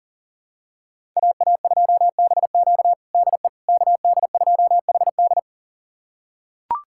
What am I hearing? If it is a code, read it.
Morse code, heard as a keyed tone, about 40 wpm, 700 Hz. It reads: AA2BQ DE KD2HD
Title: USA Callsign DE Callsign - 5 Characters - 40wpm